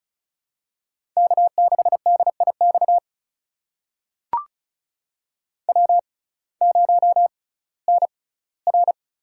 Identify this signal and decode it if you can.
Morse code, heard as a keyed tone, about 36 wpm, spaced as for 19 wpm, 700 Hz. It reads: K6DIX W0NR
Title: USA Call Signs - 6x Character Spacing - 35wpm